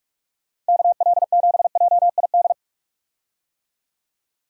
Morse code, heard as a keyed tone, about 46 wpm, 700 Hz. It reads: KL7JID
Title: DXCC-100 Call Signs - 6 Characters - Repeated 2x - 45wpm